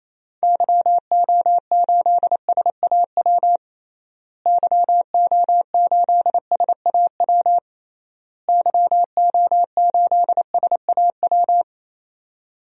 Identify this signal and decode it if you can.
Morse code, heard as a keyed tone, about 28 wpm, 700 Hz. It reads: YO8SAW YO8SAW YO8SAW
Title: DXCC-100 Call Signs - 6 Characters - Repeated 3x - 28wpm